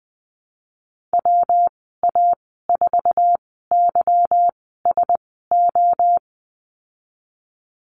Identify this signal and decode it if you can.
Morse code, heard as a keyed tone, about 20 wpm, spaced as for 15 wpm, 700 Hz. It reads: WA4YSO
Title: 161 Lesson 26g - Letter g - Call Signs 20wpm